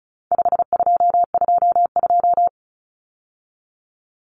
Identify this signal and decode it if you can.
Morse code, heard as a keyed tone, about 34 wpm, 700 Hz. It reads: 5222
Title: Numbers - 4-Digit 35wpm